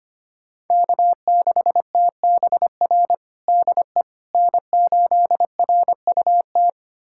Morse code, heard as a keyed tone, about 25 wpm, 700 Hz. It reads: K6TBR DE N8RUT
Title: USA Callsign DE Callsign - 5 Characters - 25wpm